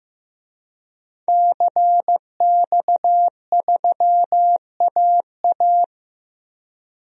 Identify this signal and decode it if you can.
Morse code, heard as a keyed tone, about 15 wpm, 700 Hz. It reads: CX3AA
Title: DXCC-100 Call Signs - 5 Characters - 15wpm